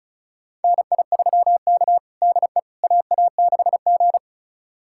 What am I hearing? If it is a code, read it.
Morse code, heard as a keyed tone, about 36 wpm, 700 Hz. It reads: NI3K DE AA6G
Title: USA Callsign DE Callsign - 4 Characters - 35wpm